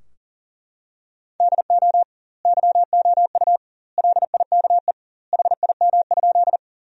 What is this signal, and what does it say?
Morse code, heard as a keyed tone, about 37 wpm, 700 Hz. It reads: DO YOU LIKE HIM?
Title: 2x Word-Spacing - Sentences from Top 100 Words - 40wpm